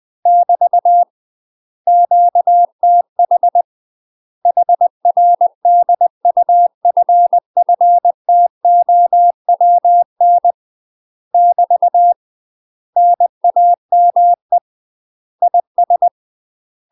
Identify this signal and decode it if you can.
Morse code, heard as a keyed tone, about 20 wpm, 700 Hz. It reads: <BT> QTH HR DUFFTOWN <BT> NAME IS
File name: UK MASTER QSOs TRAINING / 20WPM / UK MASTER QSOs 0052